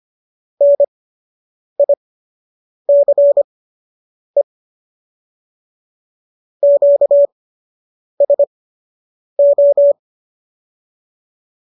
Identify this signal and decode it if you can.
Morse code, heard as a keyed tone, about 25 wpm, spaced as for 8 wpm, 570 Hz. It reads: NICE QSO